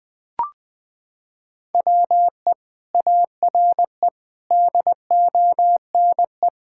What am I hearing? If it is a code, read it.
Morse code, heard as a keyed tone, about 21 wpm, 700 Hz. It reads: WE ARE DONE
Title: Sentences from Top 1600 Words 20wpm